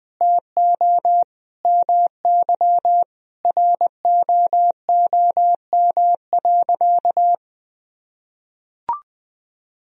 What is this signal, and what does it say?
Morse code, heard as a keyed tone, about 20 wpm, 700 Hz. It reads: TO MY ROOM.
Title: Sentences from Top 1000 Words - Review Part 3 - 20wpm